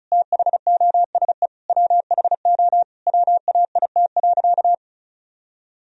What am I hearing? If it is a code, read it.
Morse code, heard as a keyed tone, about 36 wpm, 700 Hz. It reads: THOSE WHO WAIT.
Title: Sentences from Top 1000 Words - Review Part 2 - 35wpm